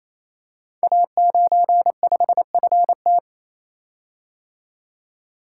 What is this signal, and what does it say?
Morse code, heard as a keyed tone, about 29 wpm, 700 Hz. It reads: A95FT
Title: Code Groups - 5 Letter:Numbers - 28wpm